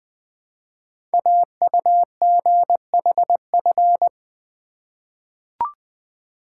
Morse code, heard as a keyed tone, about 20 wpm, 700 Hz. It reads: AUGHF
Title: Code Groups - 5 Letters - 20wpm